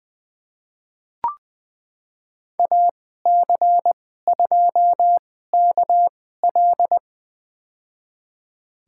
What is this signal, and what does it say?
Morse code, heard as a keyed tone, about 21 wpm, spaced as for 15 wpm, 700 Hz. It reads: AC2KL
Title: USA Call Signs - 5 Characters - 2x Character Spacing - 20wpm